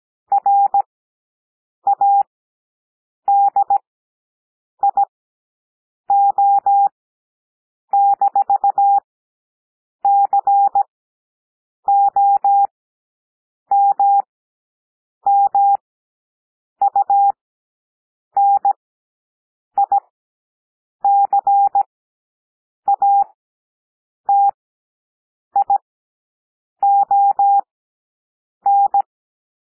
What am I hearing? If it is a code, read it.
Morse code, heard as a keyed tone, about 17 wpm, spaced as for 7 wpm, 800 Hz. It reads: RADIO-COMMUNICATION
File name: SOUND PLAIN 17 wpm